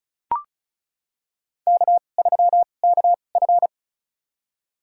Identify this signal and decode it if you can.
Morse code, heard as a keyed tone, about 36 wpm, spaced as for 25 wpm, 700 Hz. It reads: K3KF
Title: USA Call Signs - 2x Character Spacing - 35wpm